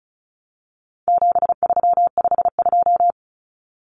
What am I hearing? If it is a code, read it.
Morse code, heard as a keyed tone, about 34 wpm, 700 Hz. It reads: 7352
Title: Numbers - 4-Digit 35wpm